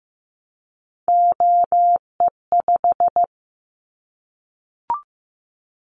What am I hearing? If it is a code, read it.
Morse code, heard as a keyed tone, about 15 wpm, 700 Hz. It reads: OE5
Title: Callsign Prefixes for Top 25 DXCC - 15wpm